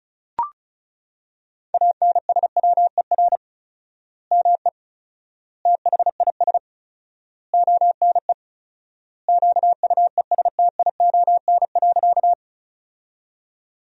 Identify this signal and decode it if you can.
Morse code, heard as a keyed tone, about 34 wpm, 700 Hz. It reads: ANSWER ME THIS ONE QUESTION.
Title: 4x Word-Spacing - Sentences from Top 400 Words - 35wpm